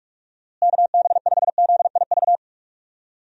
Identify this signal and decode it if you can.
Morse code, heard as a keyed tone, about 42 wpm, 700 Hz. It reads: KB5ZIV
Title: USA Call Signs - 6 Characters - Repeated 3x - 45wpm